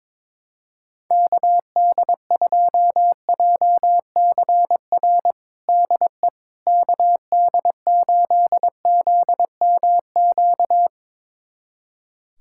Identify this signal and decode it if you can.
Morse code, heard as a keyed tone, about 22 wpm, 700 Hz. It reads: KD2JCR DE KD8ZMQ
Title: USA Callsign DE Callsign 22wpm